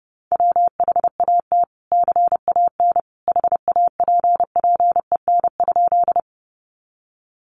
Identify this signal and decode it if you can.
Morse code, heard as a keyed tone, about 30 wpm, 700 Hz. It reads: WHAT CAN HAPPEN?
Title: Sentences from Top 400 Words 30wpm